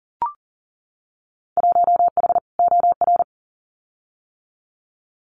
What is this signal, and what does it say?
Morse code, heard as a keyed tone, about 41 wpm, 700 Hz. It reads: 1H OR
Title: Winter Field Day Exchanges 40wpm